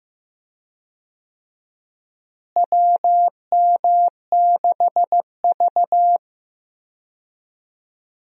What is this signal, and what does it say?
Morse code, heard as a keyed tone, about 15 wpm, 700 Hz. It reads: WM6V
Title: USA Call Signs - 4 Characters - Repeated 2x - 15wpm